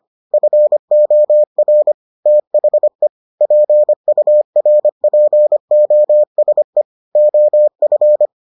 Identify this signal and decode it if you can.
Morse code, heard as a keyed tone, about 25 wpm, 600 Hz. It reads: FOR THE PURPOSE OF